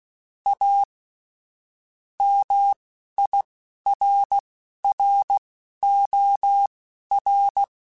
Morse code, heard as a keyed tone, about 16 wpm, spaced as for 12 wpm, 800 Hz. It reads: A MIRROR
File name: Technical Subjects / Technical Subjects-mp3s / OR Gate @10